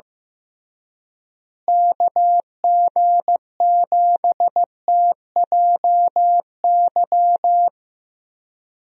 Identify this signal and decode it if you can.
Morse code, heard as a keyed tone, about 15 wpm, 700 Hz. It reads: KG7TJY
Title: USA Call Signs - 6 Characters - Repeated 3x - 15wpm